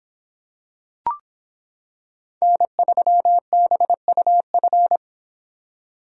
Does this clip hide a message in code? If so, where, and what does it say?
Yes: Morse code from 2.4 to 5.0 s, heard as a keyed tone, about 25 wpm, 700 Hz. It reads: N3BUF